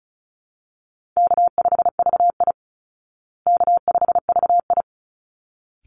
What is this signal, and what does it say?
Morse code, heard as a keyed tone, about 34 wpm, 700 Hz. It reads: K5VI K5VI
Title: Easy Call Signs Repeated 2x - 35wpm